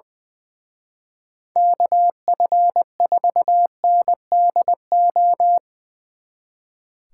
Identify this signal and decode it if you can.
Morse code, heard as a keyed tone, about 21 wpm, 700 Hz. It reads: KF4NDO
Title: Speed Racing - USA 2x3 Call Signs 20wpm